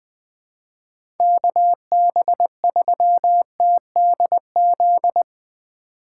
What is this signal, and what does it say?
Morse code, heard as a keyed tone, about 20 wpm, 700 Hz. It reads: KB3TDZ
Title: USA Call Signs - 6 Characters - 8x Character Spacing - 20wpm